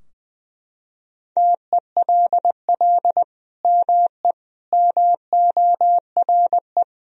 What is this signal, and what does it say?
Morse code, heard as a keyed tone, about 20 wpm, 700 Hz. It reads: TELL ME MORE
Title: Sentences from Top 1000 Words - Review Part 4 - 20wpm